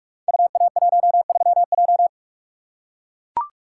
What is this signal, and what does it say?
Morse code, heard as a keyed tone, about 44 wpm, 700 Hz. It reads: UA13J
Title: Code Groups - 5 Letter:Numbers - 45wpm